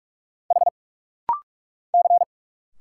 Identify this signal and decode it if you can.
Morse code, heard as a keyed tone, about 46 wpm, 690 Hz. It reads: H C
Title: Single Letters - Rapid-Fire 45wpm